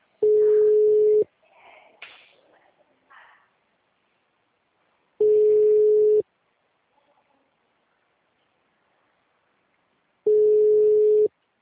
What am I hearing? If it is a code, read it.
Ringback tone (Europe, 425 Hz), with speech.
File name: out- -2004- - - .12862